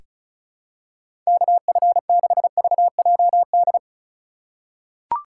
Morse code, heard as a keyed tone, about 36 wpm, 700 Hz. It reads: KF6VJD
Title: USA Call Signs - 6 Characters - 3x Character Spacing - 35wpm